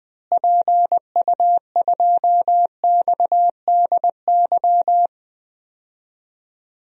Morse code, heard as a keyed tone, about 21 wpm, 700 Hz. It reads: PU2XDY
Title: DXCC-100 Call Signs - 6 Characters - Repeated 2x - 20wpm